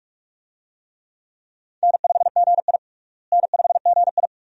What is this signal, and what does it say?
Morse code, heard as a keyed tone, about 46 wpm, 700 Hz. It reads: N5GI N5GI
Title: USA Call Signs - 4 Characters - Repeated 2x - 45wpm